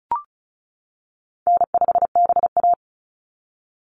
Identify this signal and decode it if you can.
Morse code, heard as a keyed tone, about 34 wpm, 700 Hz. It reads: N5BA
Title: Call Signs - 35wpm